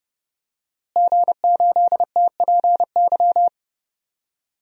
Morse code, heard as a keyed tone, about 30 wpm, 700 Hz. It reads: G8TPY
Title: DXCC-100 Call Signs - 5 Characters - Repeated 3x - 30wpm